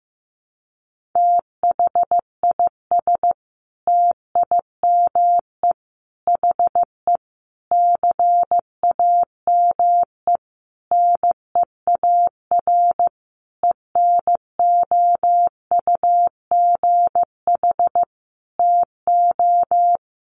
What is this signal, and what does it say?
Morse code, heard as a keyed tone, about 15 wpm, 700 Hz. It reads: THIS TIME HE CAME NEAR ENOUGH TO
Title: Sentences from Top 1200 Words 15wpm